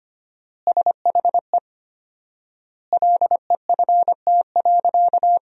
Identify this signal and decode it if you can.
Morse code, heard as a keyed tone, about 25 wpm, 700 Hz. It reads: SHE LEFT.
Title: 4x Word-Spacing - Sentences from Top 300 Words - 25wpm